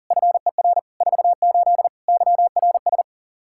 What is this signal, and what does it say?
Morse code, heard as a keyed tone, about 40 wpm, 700 Hz. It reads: FER 48 YRS
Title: Practice QSO - Exchange 3 - More Info - 40wpm